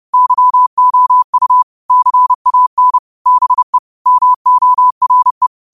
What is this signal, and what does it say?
Morse code, heard as a keyed tone, about 30 wpm, 1000 Hz. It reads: YOU CAN BE MORE
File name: qotdfiles / 30wpm